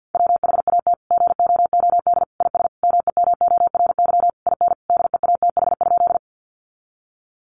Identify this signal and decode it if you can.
Morse code, heard as a keyed tone, about 50 wpm, 700 Hz. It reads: WHAT GOOD IS MEMORY IN DEATH?
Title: Sentences from Top 1300 Words 50wpm